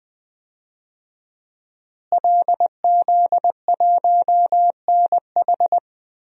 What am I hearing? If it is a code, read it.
Morse code, heard as a keyed tone, about 20 wpm, 700 Hz. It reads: LZ1NH
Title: DXCC-100 Call Signs - 5 Characters - Repeated 2x - 20wpm